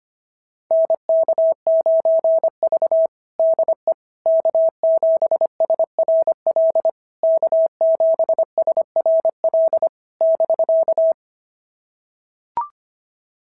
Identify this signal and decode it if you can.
Morse code, heard as a keyed tone, about 25 wpm, 650 Hz. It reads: NK9V DE K7SRL K7SRL <BK>
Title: SOTA QSO Practice 25wpm